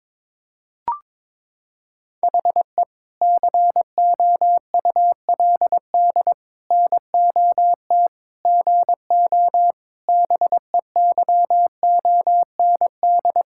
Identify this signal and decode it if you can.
Morse code, heard as a keyed tone, about 22 wpm, 700 Hz. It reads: HE COULD NOT GO BEYOND
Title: Sentences from Top 1000 Words - Review Part 6 - 22wpm